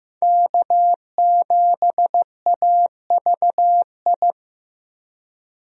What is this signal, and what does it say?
Morse code, heard as a keyed tone, about 15 wpm, 700 Hz. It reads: K7AVI
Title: USA Call Signs - 5 Characters - Repeated 2x - 15wpm